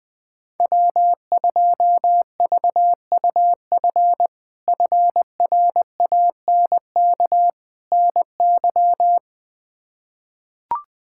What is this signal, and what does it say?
Morse code, heard as a keyed tone, about 20 wpm, 700 Hz. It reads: W2VUF FRANK NY
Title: K1USN SST QSO Practice - 20wpm